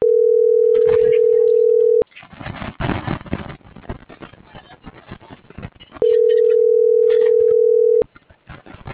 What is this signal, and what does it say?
Ringback tone (North America), 440 + 480 Hz.
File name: CENTRAL TERRESTRE / out claudiacruz